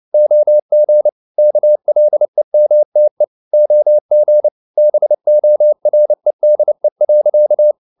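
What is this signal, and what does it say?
Morse code, heard as a keyed tone, about 29 wpm, 600 Hz. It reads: OG KLEMTE OG BOREDE.
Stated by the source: Fast (QRQ) Morse at 29 wpm, from CW med Gnister.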